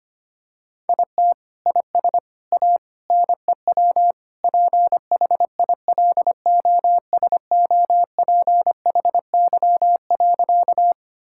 Morse code, heard as a keyed tone, about 25 wpm, 700 Hz. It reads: IT IS A NEW PHILOSOPHY.
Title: Sentences from Top 1000 Words - Review Part 6 - 25wpm